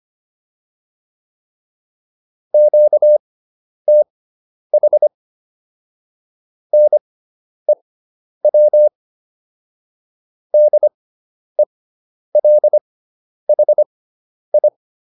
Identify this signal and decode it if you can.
Morse code, heard as a keyed tone, about 25 wpm, spaced as for 10 wpm, 600 Hz. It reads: QTH NEW DELHI
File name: QSO 410 10